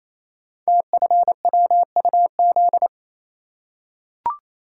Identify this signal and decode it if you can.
Morse code, heard as a keyed tone, about 27 wpm, 700 Hz. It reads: TFWUZ
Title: Code Groups - 5 Letter:Numbers - 28wpm